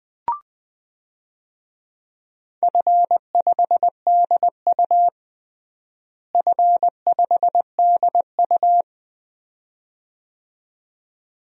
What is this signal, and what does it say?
Morse code, heard as a keyed tone, about 20 wpm, 700 Hz. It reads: F5DU F5DU
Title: DXCC-100 Call Signs - 4 Characters - Repeated 2x - 20wpm